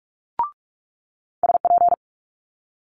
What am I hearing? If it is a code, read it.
Morse code, heard as a keyed tone, about 46 wpm, 700 Hz. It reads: SP
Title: Multi-letter Phonemes Quick Fire 45wpm